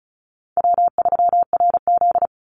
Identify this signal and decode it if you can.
Morse code, heard as a keyed tone, about 34 wpm, 700 Hz. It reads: W3RZ
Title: Easy Call Signs Repeated 3x - 35wpm